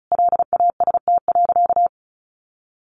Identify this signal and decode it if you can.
Morse code, heard as a keyed tone, about 34 wpm, 700 Hz. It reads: LAST.
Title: Sentences from Top 1200 Words 35wpm